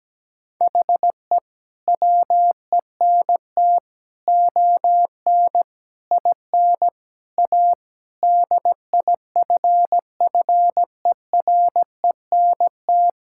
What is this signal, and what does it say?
Morse code, heard as a keyed tone, about 17 wpm, 700 Hz. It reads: HE WENT ON IN A DIFFERENT
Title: Sentences from Top 1600 Words 17wpm